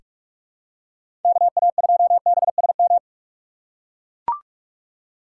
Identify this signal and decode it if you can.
Morse code, heard as a keyed tone, about 43 wpm, 700 Hz. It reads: KA2BSM
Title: USA Call Signs - 3x Character Spacing - 45wpm